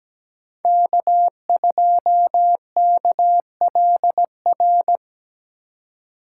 Morse code, heard as a keyed tone, about 17 wpm, 700 Hz. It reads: K2KLR
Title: USA Call Signs - 5 Characters - 4x Character Spacing - 17wpm